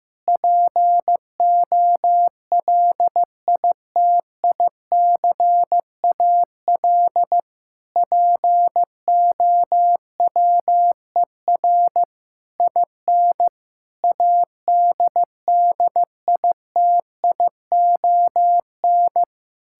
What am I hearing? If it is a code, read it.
Morse code, heard as a keyed tone, about 15 wpm, 700 Hz. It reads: POLITICAL POWER IN ADDITION
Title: Sentences from Top 1000 Words - Review Part 6 - 15wpm